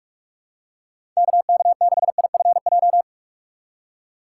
Morse code, heard as a keyed tone, about 43 wpm, 700 Hz. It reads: KK6IFJ
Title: USA Call Signs - 6 Characters - 3x Character Spacing - 45wpm